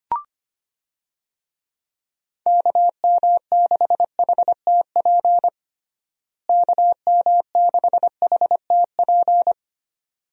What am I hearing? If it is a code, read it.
Morse code, heard as a keyed tone, about 25 wpm, 700 Hz. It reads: KM6HTP KM6HTP
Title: USA Call Signs - 6 Characters - Repeated 2x - 25wpm